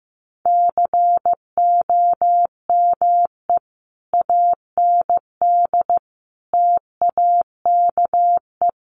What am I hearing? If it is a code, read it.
Morse code, heard as a keyed tone, about 15 wpm, 700 Hz. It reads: COME AND TAKE
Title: Sentences from Top 1500 Words 15wpm